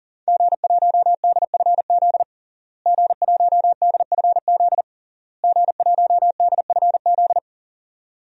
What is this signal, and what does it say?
Morse code, heard as a keyed tone, about 40 wpm, 700 Hz. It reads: G1DFZ G1DFZ G1DFZ
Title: DXCC-100 Call Signs - 5 Characters - Repeated 3x - 40wpm